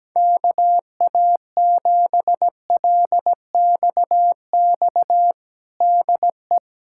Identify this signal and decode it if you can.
Morse code, heard as a keyed tone, about 17 wpm, 700 Hz. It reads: KA7LXX DE
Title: USA Callsign DE Callsign 17wpm